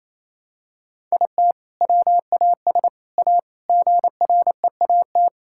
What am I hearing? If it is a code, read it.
Morse code, heard as a keyed tone, about 28 wpm, 700 Hz. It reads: IT WAS A GREAT
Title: Sentences from Top 1000 Words - Review Part 5 - 28wpm